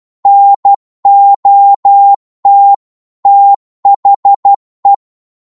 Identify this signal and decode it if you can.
Morse code, heard as a keyed tone, about 12 wpm, 800 Hz. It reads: NOT THE